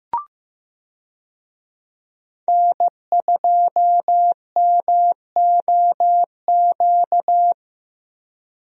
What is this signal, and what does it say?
Morse code, heard as a keyed tone, about 15 wpm, 700 Hz. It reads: N2MOQ
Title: USA Call Signs - 5 Characters - 15wpm